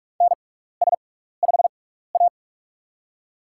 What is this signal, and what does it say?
Morse code, heard as a keyed tone, about 44 wpm, spaced as for 15 wpm, 700 Hz. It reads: NS5A
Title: USA Call Signs - 4 Characters - 6x Character Spacing - 45wpm